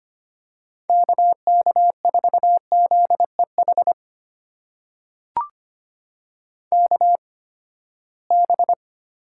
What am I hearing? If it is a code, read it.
Morse code, heard as a keyed tone, about 25 wpm, spaced as for 17 wpm, 700 Hz. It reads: KK4ZEH KB
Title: USA Call Signs - 6 Characters - 8x Character Spacing - 25wpm